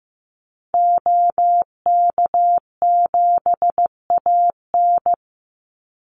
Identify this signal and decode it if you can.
Morse code, heard as a keyed tone, about 15 wpm, 700 Hz. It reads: OK7AN
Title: Call Signs Repeated 2x - 15wpm